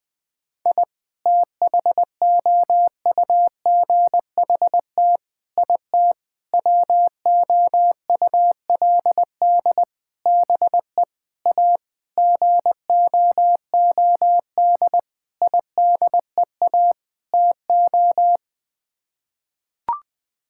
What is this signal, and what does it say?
Morse code, heard as a keyed tone, about 21 wpm, 700 Hz. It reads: I THOUGHT IT WOULD BE A GOOD IDEA TO
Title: Sets of 9 Words - 20wpm